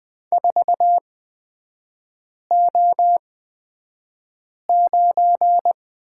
Morse code, heard as a keyed tone, about 20 wpm, spaced as for 5 wpm, 700 Hz. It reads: 4O9
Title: 250 Lesson 40e - BK - Three Characters 20wpm